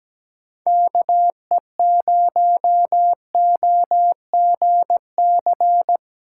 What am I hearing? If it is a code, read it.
Morse code, heard as a keyed tone, about 20 wpm, 700 Hz. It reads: KE0OGC
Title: Speed Racing - USA 2x3 Call Signs 17wpm